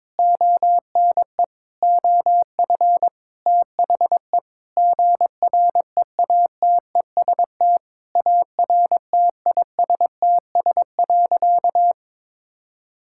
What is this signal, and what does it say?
Morse code, heard as a keyed tone, about 22 wpm, 700 Hz. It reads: ONE OF THE GREATEST ARTISTS.
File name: practice / 22wpm / Sentences with 8-Word Phrases - 22wpm